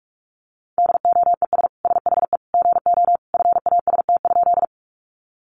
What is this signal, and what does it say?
Morse code, heard as a keyed tone, about 44 wpm, 700 Hz. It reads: DOES SHE GO FAST?
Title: Sentences from Top 400 Words 45wpm